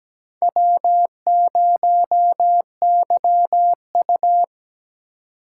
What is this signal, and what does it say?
Morse code, heard as a keyed tone, about 17 wpm, 700 Hz. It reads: W0YU
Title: USA Call Signs - 4x Character Spacing - 17wpm